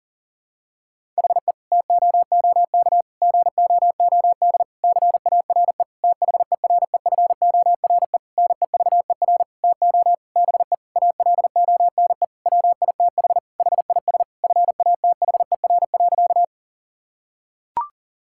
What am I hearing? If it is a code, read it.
Morse code, heard as a keyed tone, about 40 wpm, 700 Hz. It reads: HE TOOK GOOD CARE THEREFORE NEVER TO BE ALONE WITH HIS FATHER.
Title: Sentences from Top 1000 Words - Review Part 5 - 40wpm